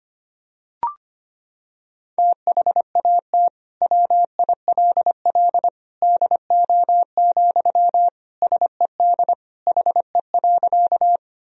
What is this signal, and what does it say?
Morse code, heard as a keyed tone, about 25 wpm, 700 Hz. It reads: THAT WILL DO, SED HE.
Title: Sentences from Top 100 Words and Abbreviations 25wpm